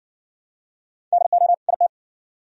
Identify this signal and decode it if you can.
Morse code, heard as a keyed tone, about 58 wpm, 700 Hz. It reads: DO IT